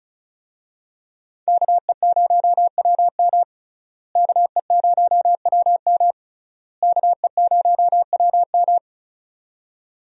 Morse code, heard as a keyed tone, about 36 wpm, 700 Hz. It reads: KE0WM KE0WM KE0WM
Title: USA Call Signs - 5 Characters - Repeated 3x - 35wpm